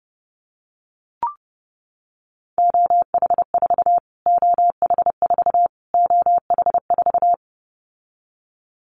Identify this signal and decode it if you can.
Morse code, heard as a keyed tone, about 28 wpm, 700 Hz. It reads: OH4 OH4 OH4
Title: Callsign Prefixes for Top 25 DXCC - Repeated 3x - 30wpm